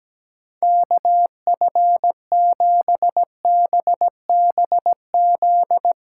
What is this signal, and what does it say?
Morse code, heard as a keyed tone, about 17 wpm, 700 Hz. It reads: KF7BBZ
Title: USA Callsign DE Callsign - 6 Characters - 17wpm